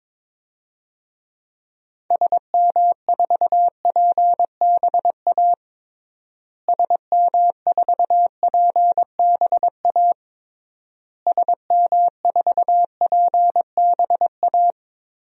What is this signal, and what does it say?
Morse code, heard as a keyed tone, about 22 wpm, 700 Hz. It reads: SM4PBA SM4PBA SM4PBA
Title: DXCC-100 Call Signs - 6 Characters - Repeated 3x - 22wpm